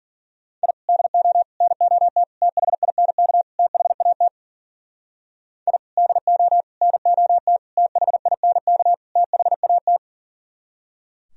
Morse code, heard as a keyed tone, about 50 wpm, 700 Hz. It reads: I DO NOT THINK THAT I DO NOT THINK THAT
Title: Speed-Racing - Sets of 5 Words 40wpm